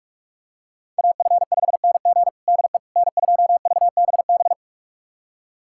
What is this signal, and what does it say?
Morse code, heard as a keyed tone, about 46 wpm, 700 Hz. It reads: AF5NG DE N2VBB
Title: USA Callsign DE Callsign - 5 Characters - 45wpm